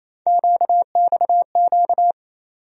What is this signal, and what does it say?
Morse code, heard as a keyed tone, about 28 wpm, 700 Hz. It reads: QXQ
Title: USA Call Sign Postfixes - 3 Letters 28wpm